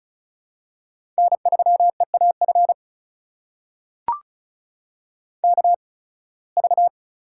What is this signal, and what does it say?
Morse code, heard as a keyed tone, about 34 wpm, spaced as for 23 wpm, 700 Hz. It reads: N3EAF KV
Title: USA Call Signs - 5 Characters - 8x Character Spacing - 35wpm